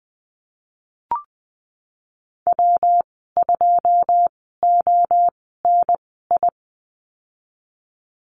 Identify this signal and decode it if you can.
Morse code, heard as a keyed tone, about 20 wpm, spaced as for 15 wpm, 700 Hz. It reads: W2ONI
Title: 122 Lesson 20g - Number 3 - Call Signs 20wpm